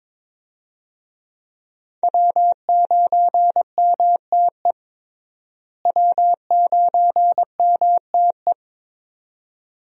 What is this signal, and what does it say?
Morse code, heard as a keyed tone, about 22 wpm, 700 Hz. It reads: W9MTE W9MTE
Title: USA Call Signs - 5 Characters - Repeated 2x - 22wpm